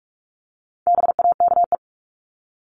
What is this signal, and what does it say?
Morse code, heard as a keyed tone, about 44 wpm, 700 Hz. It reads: BAKE
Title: Top 1751-2000 Words 45wpm